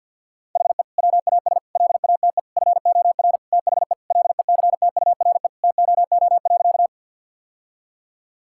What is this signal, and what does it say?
Morse code, heard as a keyed tone, about 50 wpm, 700 Hz. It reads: HE WAS LATE FOR THE LECTURE TOO.
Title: Sentences from Top 1600 Words 50wpm